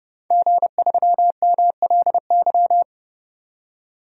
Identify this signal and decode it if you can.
Morse code, heard as a keyed tone, about 30 wpm, 700 Hz. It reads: G3MLY
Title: DXCC-100 Call Signs - 5 Characters - 30wpm